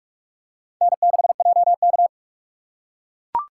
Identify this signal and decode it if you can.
Morse code, heard as a keyed tone, about 40 wpm, 700 Hz. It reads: N6JK